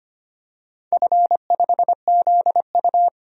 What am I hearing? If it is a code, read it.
Morse code, heard as a keyed tone, about 25 wpm, 700 Hz. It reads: F5ZU
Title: DXCC-100 Call Signs - 4 Characters - 25wpm